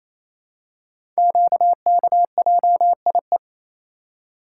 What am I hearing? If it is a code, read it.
Morse code, heard as a keyed tone, about 27 wpm, 700 Hz. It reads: QKJIE
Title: Code Groups - 5 Letters - 28wpm